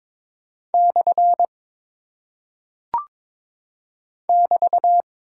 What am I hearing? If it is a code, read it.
Morse code, heard as a keyed tone, about 22 wpm, 700 Hz. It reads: / <BT>
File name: practice / 22wpm / Single Punctuation-Prosign - 22wpm